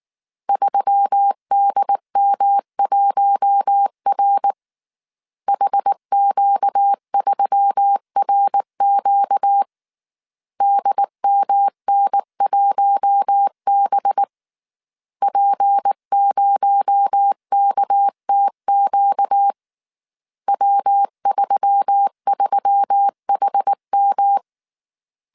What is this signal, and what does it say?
Morse code, heard as a keyed tone, about 19 wpm, 800 Hz. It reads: 3DM1R HQ3RQ DMN1B P0KTQ W33HM
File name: CW / DET / 1 6 1500C S A 18wpm 2255